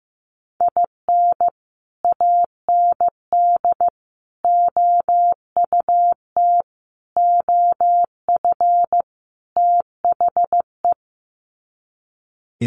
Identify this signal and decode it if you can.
Morse code, heard as a keyed tone, about 15 wpm, 700 Hz. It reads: IN AND OUT OF THE
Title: Sets of 5 Words 15wpm